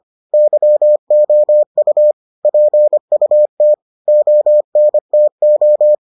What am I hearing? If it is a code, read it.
Morse code, heard as a keyed tone, about 25 wpm, 600 Hz. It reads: YOU PUT ONTO